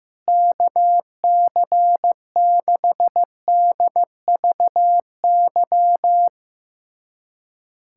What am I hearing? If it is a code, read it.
Morse code, heard as a keyed tone, about 15 wpm, 700 Hz. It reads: KC6DVY
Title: USA Callsign DE Callsign 15wpm